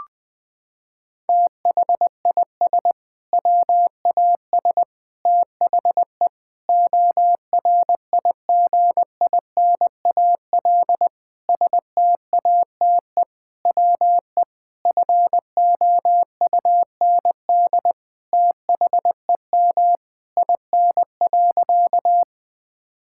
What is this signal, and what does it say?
Morse code, heard as a keyed tone, about 21 wpm, 700 Hz. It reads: THIS WAS THE ORIGINAL STATE WE FOUND THEM IN.
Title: Sentences from Top 1000 Words - Review Part 6 - 20wpm